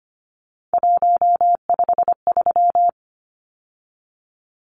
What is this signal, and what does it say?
Morse code, heard as a keyed tone, about 25 wpm, 690 Hz. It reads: 153
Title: Numbers - 3-Digit 25wpm